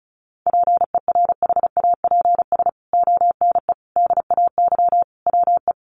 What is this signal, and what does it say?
Morse code, heard as a keyed tone, about 34 wpm, 700 Hz. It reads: PERHAPS ONE DAY WE
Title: Sentences from Top 600 Words 35wpm